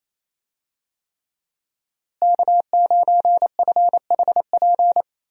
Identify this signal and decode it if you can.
Morse code, heard as a keyed tone, about 28 wpm, 700 Hz. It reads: K9FHP
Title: USA Call Signs - 5 Characters - Repeated 3x - 28wpm